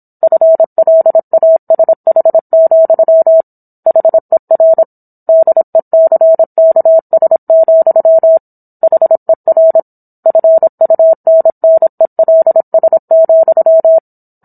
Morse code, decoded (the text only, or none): FLASH, HER DECKS, HER FUNNELS,